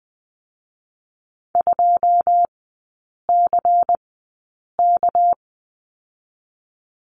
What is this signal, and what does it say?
Morse code, heard as a keyed tone, about 19 wpm, spaced as for 8 wpm, 700 Hz. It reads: 2CK
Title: 218 Lesson 35e - Number 0 - Three Characters 20wpm